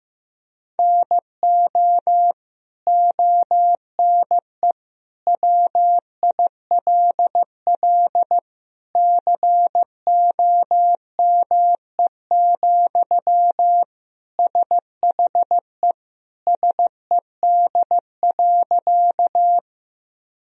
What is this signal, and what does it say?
Morse code, heard as a keyed tone, about 15 wpm, 700 Hz. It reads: NO ONE WILL COME, SHE SED.
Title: Sentences from Top 100 Words and Abbreviations 15wpm